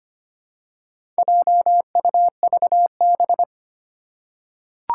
Morse code, heard as a keyed tone, about 25 wpm, 700 Hz. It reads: JUVB
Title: Four Challenging Letter-Number Combinations 25wpm